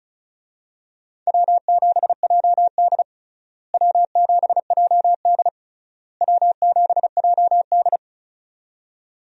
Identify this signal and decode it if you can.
Morse code, heard as a keyed tone, about 34 wpm, 700 Hz. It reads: W7JD W7JD W7JD
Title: USA Call Signs - 4 Characters - Repeated 3x - 35wpm